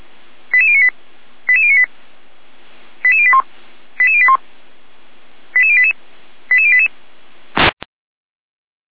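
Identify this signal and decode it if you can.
Five-tone selective call (ZVEI), heard as quick runs of five tones, tones of about 70 ms. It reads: 88098 88098 88081 88081 88088 88088
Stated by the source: ZVEI five-tone selective call, address 88571, received by SDR.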